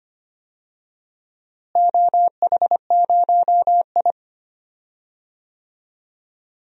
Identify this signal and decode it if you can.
Morse code, heard as a keyed tone, about 25 wpm, 700 Hz. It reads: OH0I
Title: DXCC-100 Call Signs - 4 Characters - Repeated 3x - 25wpm